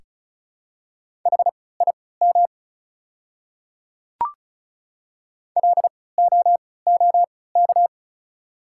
Morse code, heard as a keyed tone, about 36 wpm, spaced as for 20 wpm, 690 Hz. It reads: HIM LOOK
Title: 3x Character Spacing - Top 100 Words - 35wpm